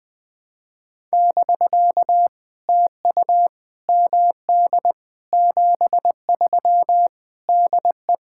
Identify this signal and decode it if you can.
Morse code, heard as a keyed tone, about 19 wpm, 700 Hz. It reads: <BK> TU MD 73 DE
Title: POTA QSO Practice - 20wpm